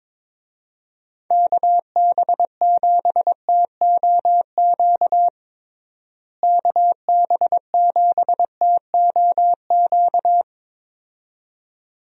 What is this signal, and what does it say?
Morse code, heard as a keyed tone, about 22 wpm, 700 Hz. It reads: KB7TOQ KB7TOQ
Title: USA Call Signs - 6 Characters - Repeated 2x - 22wpm